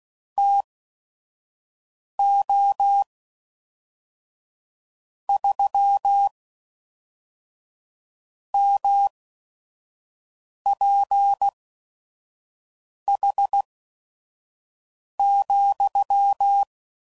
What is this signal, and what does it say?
Morse code, heard as a keyed tone, about 16 wpm, spaced as for 5 wpm, 800 Hz. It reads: TO 3 MPH,
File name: CW Code Groups @5WPM / Weather Sets / Weather 4 28